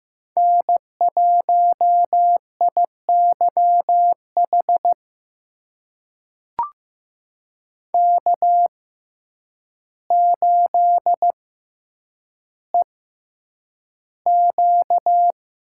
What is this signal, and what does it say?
Morse code, heard as a keyed tone, about 15 wpm, spaced as for 8 wpm, 700 Hz. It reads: N1IYH K8EQ
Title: USA Call Signs - 5 Characters - 6x Character Spacing - 15wpm